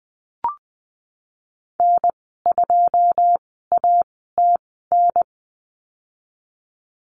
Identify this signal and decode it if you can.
Morse code, heard as a keyed tone, about 19 wpm, spaced as for 14 wpm, 700 Hz. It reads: N2ATN
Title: 088 Lesson 15g - Number 5 - Call Signs 20wpm